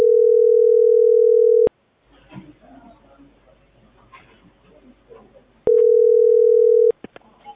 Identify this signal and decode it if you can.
Ringback tone (North America), 440 + 480 Hz.